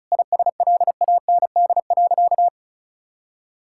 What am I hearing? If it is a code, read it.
Morse code, heard as a keyed tone, about 36 wpm, 700 Hz. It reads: ISLAND.
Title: Sentences from Top 1600 Words 35wpm